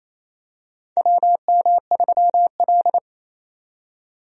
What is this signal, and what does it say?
Morse code, heard as a keyed tone, about 27 wpm, 700 Hz. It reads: WM3L
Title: USA Call Signs - 4 Characters - 28wpm